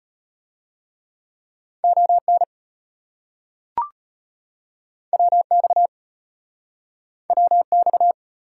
Morse code, heard as a keyed tone, about 34 wpm, 700 Hz. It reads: ON WX WX